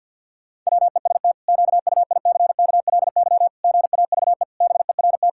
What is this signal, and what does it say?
Morse code, heard as a keyed tone, about 50 wpm, 700 Hz. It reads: WEST QUICKLY GAVE BERT
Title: Pangram Challenge v2 - 50wpm